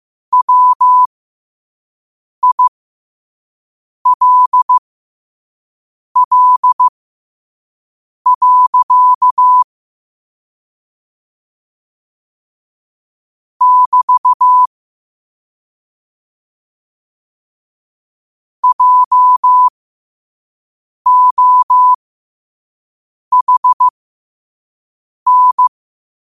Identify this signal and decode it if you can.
Morse code, heard as a keyed tone, about 15 wpm, spaced as for 5 wpm, 1000 Hz. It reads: WILL. <BT> JOHN
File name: qotdfiles / 5wpm